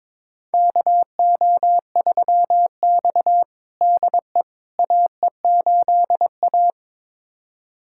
Morse code, heard as a keyed tone, about 22 wpm, 700 Hz. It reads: KO3X DE AE8A
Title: USA Callsign DE Callsign - 4 Characters - 22wpm